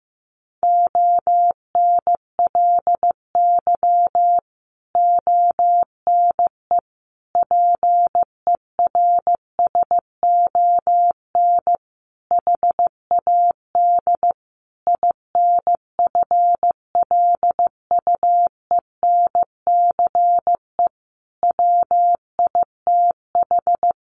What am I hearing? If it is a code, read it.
Morse code, heard as a keyed tone, about 15 wpm, 700 Hz. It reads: ONLY ONE PERSON HAD INFLUENCE WITH
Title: Sentences from Top 1100 Words 15wpm